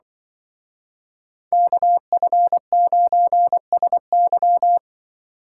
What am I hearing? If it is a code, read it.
Morse code, heard as a keyed tone, about 24 wpm, 700 Hz. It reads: KF9SY
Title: Speed Racing - USA 2x2 Call Signs 20wpm